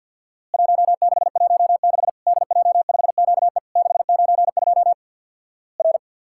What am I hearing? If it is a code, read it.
Morse code, heard as a keyed tone, about 50 wpm, 700 Hz. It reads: 1616 DJ5QE 692 R
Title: Worked All Europe DX Contest - QTC Exchange with Repeats - 50wpm